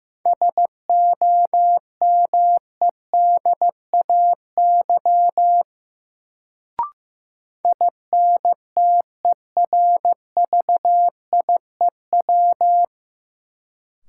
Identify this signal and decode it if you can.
Morse code, heard as a keyed tone, about 15 wpm, 700 Hz. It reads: SOMEDAY INTERVIEW
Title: Sentences from Top 1600 Words - New Word Drill Alt - 15wpm